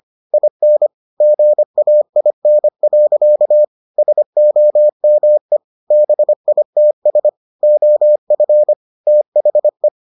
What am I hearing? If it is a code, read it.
Morse code, heard as a keyed tone, about 25 wpm, 600 Hz. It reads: IN GAIN. SOME BITS OF THE